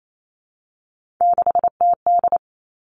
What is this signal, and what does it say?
Morse code, heard as a keyed tone, about 28 wpm, 690 Hz. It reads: 6TD
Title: Three Challenging Letter-Number Combinations 28wpm